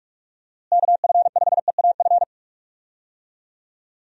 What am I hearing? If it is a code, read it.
Morse code, heard as a keyed tone, about 45 wpm, 700 Hz. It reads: KF5EAF
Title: USA Call Signs - 6 Characters - Repeated 2x - 45wpm